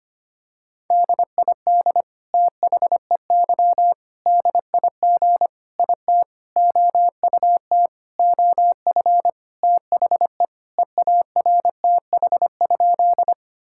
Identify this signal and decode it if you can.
Morse code, heard as a keyed tone, about 25 wpm, 700 Hz. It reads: DID THEY DIG IT OUT OF THE EARTH?
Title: Sentences from Top 1600 Words 25wpm